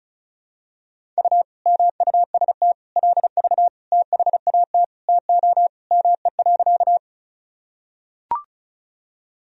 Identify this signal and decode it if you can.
Morse code, heard as a keyed tone, about 34 wpm, 700 Hz. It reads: U MUST LV THAT TO ME.
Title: Sentences from Top 400 Words and Abbreviations 35wpm